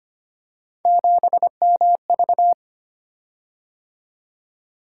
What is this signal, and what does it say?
Morse code, heard as a keyed tone, about 25 wpm, 700 Hz. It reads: 7MV
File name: Three Letter-Number Combinations - 25wpm